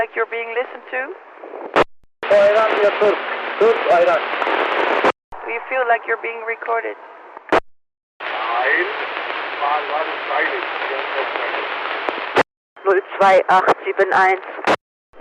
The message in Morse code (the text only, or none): none